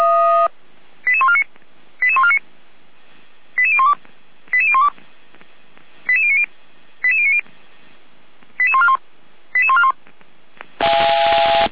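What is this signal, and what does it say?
Five-tone selective call (ZVEI), heard as quick runs of five tones, tones of about 70 ms. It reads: 88149 88149 88012 88012 88090 88090 88141 88141
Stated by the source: ZVEI five-tone selective call received by SDR, recorded by Feuerwehr Orlamünde (f85360).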